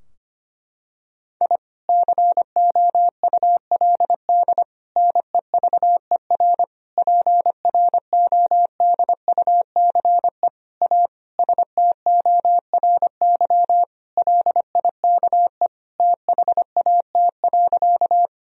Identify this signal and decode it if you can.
Morse code, heard as a keyed tone, about 25 wpm, 700 Hz. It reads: I COULD NEVER PRODUCE A STORY LIKE THAT.
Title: Sentences from Top 1000 Words - Review Part 3 - 25wpm